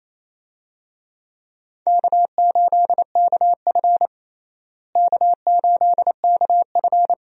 Morse code, heard as a keyed tone, about 27 wpm, 700 Hz. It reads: K8KF K8KF
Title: USA Call Signs - 4 Characters - Repeated 2x - 28wpm